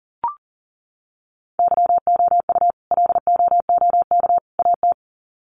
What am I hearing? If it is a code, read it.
Morse code, heard as a keyed tone, about 37 wpm, 700 Hz. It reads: YOU LOOK AT
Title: Sets of 3 Words 40wpm